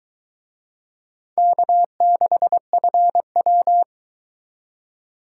Morse code, heard as a keyed tone, about 23 wpm, 700 Hz. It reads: K6FW